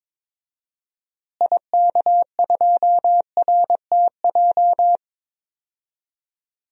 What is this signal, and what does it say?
Morse code, heard as a keyed tone, about 21 wpm, 700 Hz. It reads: IK2RTJ
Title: DXCC-100 Call Signs - 6 Characters - Repeated 2x - 22wpm